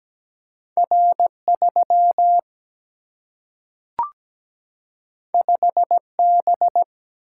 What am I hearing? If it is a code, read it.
Morse code, heard as a keyed tone, about 17 wpm, 700 Hz. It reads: R 3 5 B
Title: Single-Letter Single-Number 17wpm